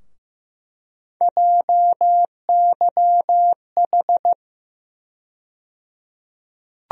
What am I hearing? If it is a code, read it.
Morse code, heard as a keyed tone, about 15 wpm, 700 Hz. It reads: JYH
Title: USA Call Sign Postfixes - 3 Letters 15wpm